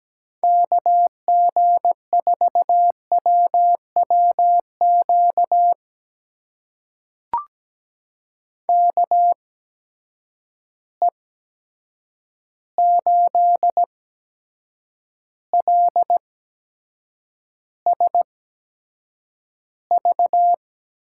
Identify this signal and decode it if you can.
Morse code, heard as a keyed tone, about 17 wpm, spaced as for 7 wpm, 700 Hz. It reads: KG4WWQ KE8LSV
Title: USA Call Signs - 6 Characters - 8x Character Spacing - 17wpm